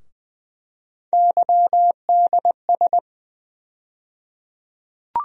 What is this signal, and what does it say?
Morse code, heard as a keyed tone, about 21 wpm, 700 Hz. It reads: YDS
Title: USA Call Sign Postfixes - 3 Letters 20wpm